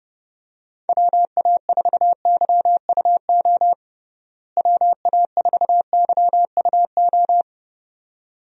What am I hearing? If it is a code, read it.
Morse code, heard as a keyed tone, about 30 wpm, 700 Hz. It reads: WA4YUO WA4YUO
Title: USA Call Signs - 6 Characters - Repeated 2x - 30wpm